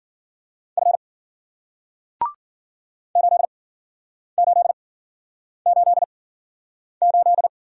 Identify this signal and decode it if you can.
Morse code, heard as a keyed tone, about 54 wpm, 700 Hz. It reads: F 8 8 8 8